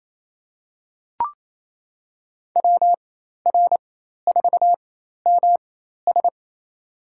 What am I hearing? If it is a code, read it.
Morse code, heard as a keyed tone, about 27 wpm, spaced as for 13 wpm, 700 Hz. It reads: WR4MS
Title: USA Call Signs - 5 Characters - 4x Character Spacing - 28wpm